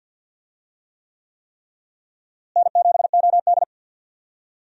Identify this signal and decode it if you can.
Morse code, heard as a keyed tone, about 51 wpm, 700 Hz. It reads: N7OD
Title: USA Call Signs - 4 Characters - Repeated 3x - 50wpm